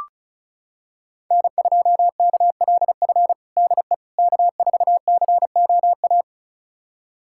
Morse code, heard as a keyed tone, about 34 wpm, 700 Hz. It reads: N2KLF DE K4COA
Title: USA Callsign DE Callsign - 5 Characters - 35wpm